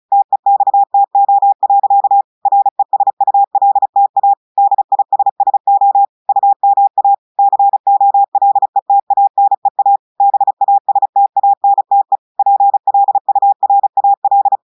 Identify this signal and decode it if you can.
Morse code, heard as a keyed tone, about 36 wpm, 800 Hz. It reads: TEXTO. RESULTA DISSO UMA COLETANEA BASTANTE PLURAL
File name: audios / portugais / portugais texte en clair 1 vitesse 35 mots minutes